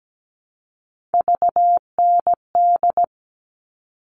Morse code, heard as a keyed tone, about 17 wpm, 700 Hz. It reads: VND
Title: All Cut Numbers - 3-Digit 17wpm